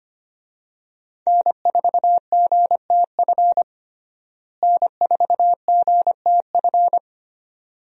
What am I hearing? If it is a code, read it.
Morse code, heard as a keyed tone, about 25 wpm, 700 Hz. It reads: N4GTF N4GTF
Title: USA Call Signs - 5 Characters - Repeated 2x - 25wpm